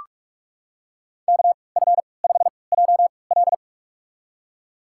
Morse code, heard as a keyed tone, about 46 wpm, spaced as for 25 wpm, 700 Hz. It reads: KF5JL
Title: USA Call Signs - 5 Characters - 3x Character Spacing - 45wpm